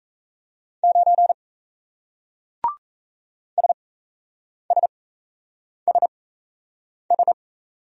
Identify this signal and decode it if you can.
Morse code, heard as a keyed tone, about 38 wpm, spaced as for 8 wpm, 700 Hz. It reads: 9 SSSS